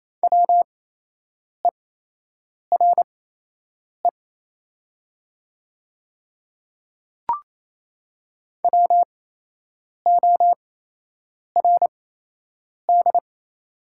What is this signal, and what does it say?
Morse code, heard as a keyed tone, about 28 wpm, spaced as for 8 wpm, 700 Hz. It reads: WERE WORD
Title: 8x Character Spacing - Top 100 Words - 28wpm